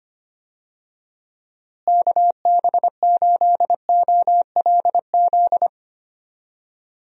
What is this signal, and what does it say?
Morse code, heard as a keyed tone, about 25 wpm, 700 Hz. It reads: KB8OLZ
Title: USA Call Signs - 6 Characters - Repeated 3x - 25wpm